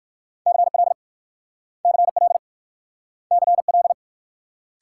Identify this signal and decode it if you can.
Morse code, heard as a keyed tone, about 54 wpm, 690 Hz. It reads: CL CL CL
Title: Speed-Racing Multi-letter Phonemes - 45wpm